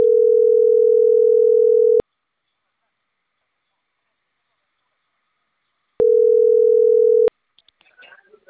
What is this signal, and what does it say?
Ringback tone (North America), 440 + 480 Hz, with speech.
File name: CENTRAL CAEA / out agcaea02